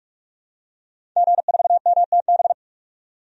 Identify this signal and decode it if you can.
Morse code, heard as a keyed tone, about 46 wpm, 700 Hz. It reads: G4MTB
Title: DXCC-100 Call Signs - 5 Characters - 45wpm